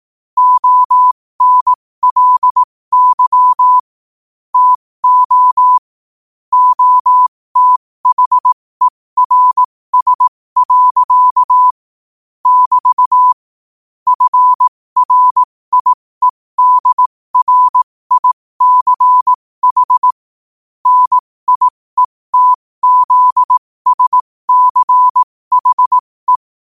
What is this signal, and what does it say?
Morse code, heard as a keyed tone, about 18 wpm, spaced as for 15 wpm, 1000 Hz. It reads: ONLY TO OTHERS. <BT> FRIEDRICH NIETZSCHE
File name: qotdfiles / 15wpm